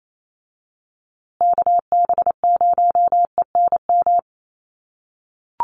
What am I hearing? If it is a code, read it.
Morse code, heard as a keyed tone, about 28 wpm, 700 Hz. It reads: KB0ENM
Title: USA Call Signs Repeated 2x 28wpm